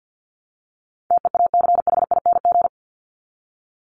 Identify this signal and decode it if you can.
Morse code, heard as a keyed tone, about 50 wpm, 700 Hz. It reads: TEACHING
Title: ING Suffix - Top 100 Words with Base Words 50wpm